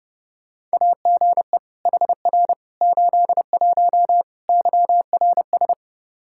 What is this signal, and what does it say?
Morse code, heard as a keyed tone, about 30 wpm, 700 Hz. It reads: AGE HR 81 YRS
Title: Practice QSO - Exchange 3 - More Info - 30wpm